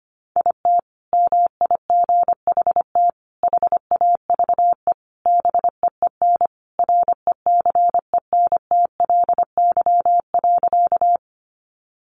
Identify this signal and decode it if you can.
Morse code, heard as a keyed tone, about 25 wpm, 700 Hz. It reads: IT MIGHT HAVE BEEN RECENTLY.
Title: Sentences from Top 1200 Words 25wpm